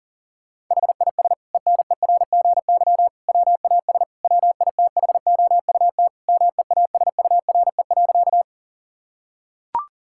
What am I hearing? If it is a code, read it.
Morse code, heard as a keyed tone, about 37 wpm, 700 Hz. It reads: HIS ENERGY WAS WITHOUT MEASURE.
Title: Sentences from Top 1000 Words - Review Part 6 - 40wpm